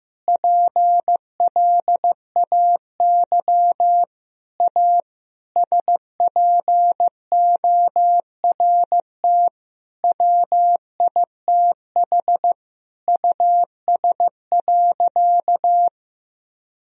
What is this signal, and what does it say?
Morse code, heard as a keyed tone, about 15 wpm, 700 Hz. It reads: PLAY A SPORT WITH US.
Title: Sentences from Top 1000 Words - Review Part 4 - 15wpm